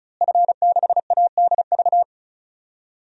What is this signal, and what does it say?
Morse code, heard as a keyed tone, about 36 wpm, 700 Hz. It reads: F6ADV
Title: DXCC-100 Call Signs - 5 Characters - Repeated 3x - 35wpm